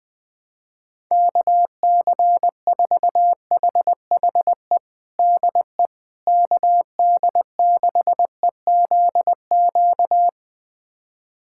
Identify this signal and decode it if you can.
Morse code, heard as a keyed tone, about 21 wpm, 700 Hz. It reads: KC4HHE DE KD6EZQ
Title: USA Callsign DE Callsign - 6 Characters - 20wpm